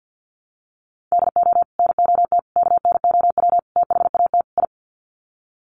Morse code, heard as a keyed tone, about 48 wpm, 700 Hz. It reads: DO NOT KNOW THAT I
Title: Sets of 5 Words Encore2 50wpm